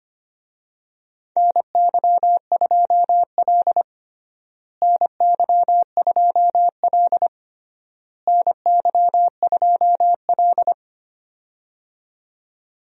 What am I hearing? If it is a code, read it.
Morse code, heard as a keyed tone, about 25 wpm, 700 Hz. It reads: NY2L NY2L NY2L
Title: USA Call Signs - 4 Characters - Repeated 3x - 25wpm